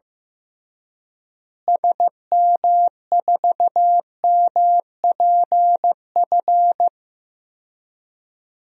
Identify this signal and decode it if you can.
Morse code, heard as a keyed tone, about 15 wpm, 700 Hz. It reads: SM4MPF
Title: DXCC-100 Call Signs - 6 Characters - Repeated 2x - 15wpm